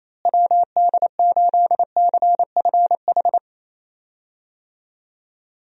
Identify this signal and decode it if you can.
Morse code, heard as a keyed tone, about 28 wpm, 700 Hz. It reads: WD8CFH
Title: USA Call Signs - 6 Characters - Repeated 2x - 28wpm